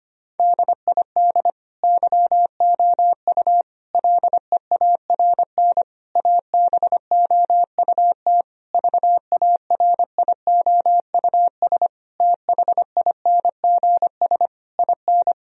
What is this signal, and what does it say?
Morse code, heard as a keyed tone, about 25 wpm, 700 Hz. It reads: DID YOU LEARN ABOUT VARIOUS THINGS IN
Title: Sentences from Top 1000 Words - Review Part 4 - 25wpm